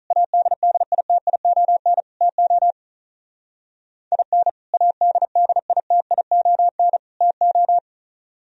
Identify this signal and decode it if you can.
Morse code, heard as a keyed tone, about 42 wpm, 700 Hz. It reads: ADDITION TO IN ADDITION TO
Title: Speed-Racing - Sets of 3 Words 35wpm